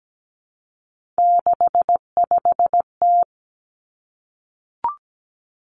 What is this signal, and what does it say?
Morse code, heard as a keyed tone, about 17 wpm, 700 Hz. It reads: 65T
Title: Three Challenging Letter-Number Combinations 17wpm